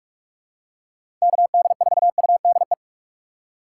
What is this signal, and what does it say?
Morse code, heard as a keyed tone, about 46 wpm, 700 Hz. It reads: KD4UDE
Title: USA Call Signs - 6 Characters - 4x Character Spacing - 45wpm